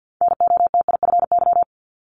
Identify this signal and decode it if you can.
Morse code, heard as a keyed tone, about 50 wpm, 690 Hz. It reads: NOTIFY
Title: Multi-letter Phonemes Repeated 3x 50wpm